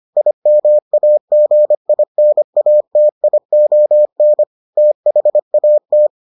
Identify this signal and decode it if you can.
Morse code, heard as a keyed tone, about 25 wpm, 600 Hz. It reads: IMAGINATION THAT